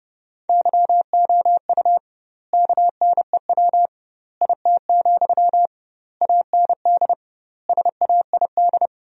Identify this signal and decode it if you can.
Morse code, heard as a keyed tone, about 30 wpm, 700 Hz. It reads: YOU KNEW IT, AND SAID
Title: 2x Word-Spacing - Sentences from Top 500 Words - 30wpm